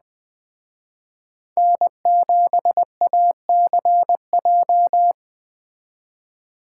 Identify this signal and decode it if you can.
Morse code, heard as a keyed tone, about 21 wpm, 700 Hz. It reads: N7ACJ